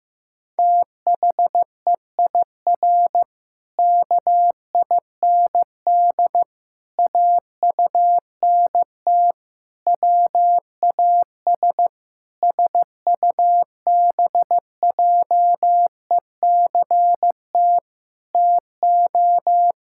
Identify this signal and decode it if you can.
Morse code, heard as a keyed tone, about 15 wpm, 700 Hz. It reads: THEIR KIND AUNT WAS SUBJECT TO
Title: Pangram Challenge v2 - 15wpm